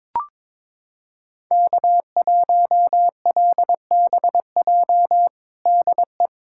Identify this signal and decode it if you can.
Morse code, heard as a keyed tone, about 21 wpm, 700 Hz. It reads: K1LBJ DE
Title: USA Callsign DE Callsign - 5 Characters - 22wpm